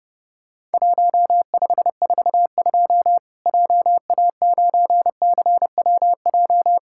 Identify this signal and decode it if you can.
Morse code, heard as a keyed tone, about 32 wpm, 700 Hz. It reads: 1542 JA9CWJ
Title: Worked All Europe DX Contest - QTC Exchange - 30wpm